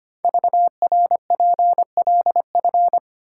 Morse code, heard as a keyed tone, about 25 wpm, 700 Hz. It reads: VRPLF
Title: Code Groups - 5 Letters - 25wpm